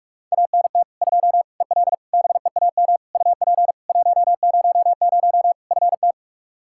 Morse code, heard as a keyed tone, about 44 wpm, 700 Hz. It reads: ANT 2 EL BEAM UP 100 FT
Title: Practice QSO - Exchange 2 - Info - 45wpm